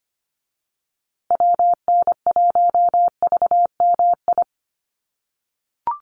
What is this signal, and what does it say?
Morse code, heard as a keyed tone, about 25 wpm, 700 Hz. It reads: WN1VMI
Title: USA Call Signs Repeated 3x 25wpm